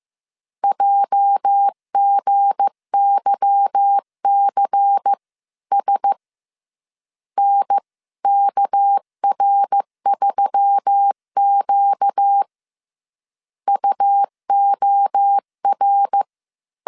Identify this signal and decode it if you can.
Morse code, heard as a keyed tone, about 15 wpm, 800 Hz. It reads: JGYCS NKR3Q UOR
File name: CW / DET / 1 9 1500C S A 14wpm 3222